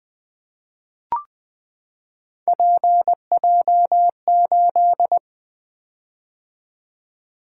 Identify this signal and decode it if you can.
Morse code, heard as a keyed tone, about 20 wpm, 700 Hz. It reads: PJ8
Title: Three Letter-Number Combinations - 20wpm